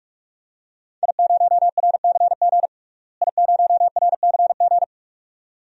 Morse code, heard as a keyed tone, about 41 wpm, 700 Hz. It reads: I0RCG I0RCG
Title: DXCC-100 Call Signs - 5 Characters - Repeated 2x - 45wpm